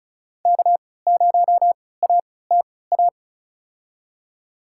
Morse code, heard as a keyed tone, about 34 wpm, spaced as for 20 wpm, 700 Hz. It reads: K0ATA
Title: USA Call Signs - 5 Characters - 3x Character Spacing - 35wpm